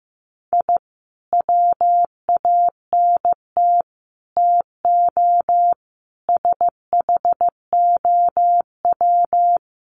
Morse code, heard as a keyed tone, about 15 wpm, 700 Hz. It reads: I WANT TO SHOW
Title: Sets of 5 Words 15wpm